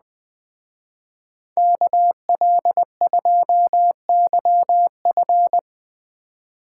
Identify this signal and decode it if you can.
Morse code, heard as a keyed tone, about 20 wpm, 700 Hz. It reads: KL2YF